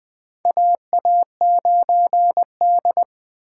Morse code, heard as a keyed tone, about 20 wpm, 700 Hz. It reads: AA9D
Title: USA Call Signs - 4 Characters - 6x Character Spacing - 20wpm